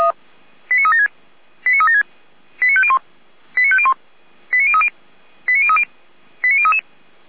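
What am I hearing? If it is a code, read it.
Five-tone selective call (ZVEI), heard as quick runs of five tones, tones of about 70 ms. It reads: 89376 89376 89501 89501 89030 89030 89033
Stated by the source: Five-tone ZVEI selective call received on an SDR.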